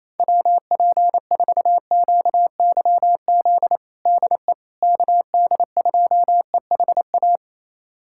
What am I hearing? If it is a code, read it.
Morse code, heard as a keyed tone, about 27 wpm, 700 Hz. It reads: WP4QYZ DE KD2EHA
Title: USA Callsign DE Callsign - 6 Characters - 28wpm